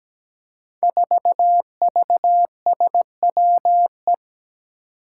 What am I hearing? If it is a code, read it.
Morse code, heard as a keyed tone, about 17 wpm, 700 Hz. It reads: 4VSWE
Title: Five Challenging Letter-Number Combinations 17wpm